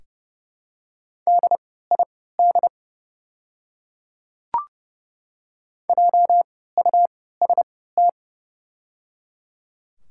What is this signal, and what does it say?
Morse code, heard as a keyed tone, about 30 wpm, spaced as for 17 wpm, 700 Hz. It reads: DID JUST
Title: 3x Character Spacing - Top 200 Words - 30wpm